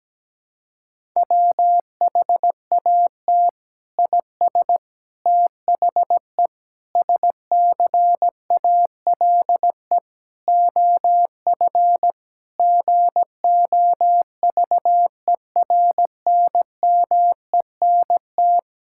Morse code, heard as a keyed tone, about 17 wpm, 700 Hz. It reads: WHAT IS THE SCALE OF GOVERNMENT
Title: Sentences from Top 1000 Words - Review Part 5 - 17wpm